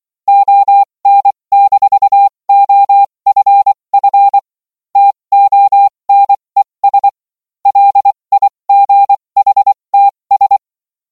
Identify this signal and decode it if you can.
Morse code, heard as a keyed tone, about 24 wpm, spaced as for 20 wpm, 800 Hz. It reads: ON-OFF TONES LIGHTS